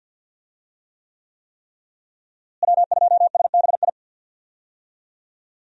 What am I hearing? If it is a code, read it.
Morse code, heard as a keyed tone, about 48 wpm, 700 Hz. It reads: W2SBI
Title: USA Call Signs - 5 Characters - Repeated 3x - 50wpm